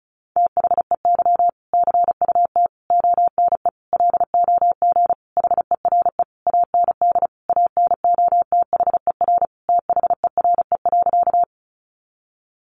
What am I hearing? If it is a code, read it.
Morse code, heard as a keyed tone, about 34 wpm, 700 Hz. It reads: THEY CUT ONE LOG HERE AND ANOTHER THERE.
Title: Sentences from Top 1400 Words 35wpm